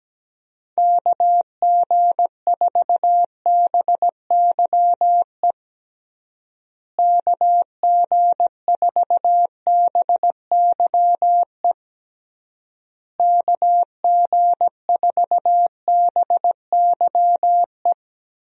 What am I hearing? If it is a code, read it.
Morse code, heard as a keyed tone, about 17 wpm, 700 Hz. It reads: KG4BYE KG4BYE KG4BYE
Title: USA Call Signs - 6 Characters - Repeated 3x - 17wpm